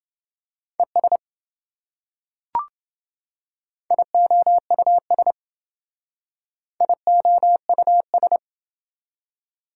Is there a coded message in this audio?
Yes: Morse code from 0.8 to 8.4 s, heard as a keyed tone, about 28 wpm, 700 Hz. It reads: ES IOUS IOUS